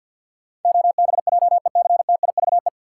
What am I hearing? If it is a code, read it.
Morse code, heard as a keyed tone, about 51 wpm, 700 Hz. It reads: OBJECTIVE
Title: Sentences from Top 1200 Words - New Word Drill - 50wpm